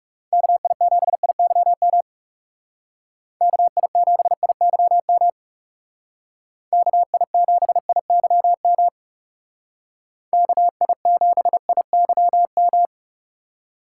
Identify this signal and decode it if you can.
Morse code, heard as a keyed tone, about 35 wpm, 700 Hz. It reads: KI7IYM KI7IYM KI7IYM KI7IYM
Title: Speed Racing - USA Call Signs 30wpm